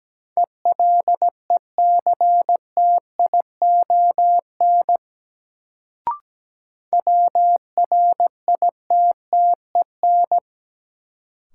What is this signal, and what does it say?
Morse code, heard as a keyed tone, about 17 wpm, 700 Hz. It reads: ELECTION WRITTEN
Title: Sentences from Top 1500 Words - New Word Drill Alt - 17wpm